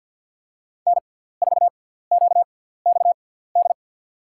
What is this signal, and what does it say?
Morse code, heard as a keyed tone, about 50 wpm, spaced as for 17 wpm, 700 Hz. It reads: N4QXD
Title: USA Call Signs - 5 Characters - 6x Character Spacing - 50wpm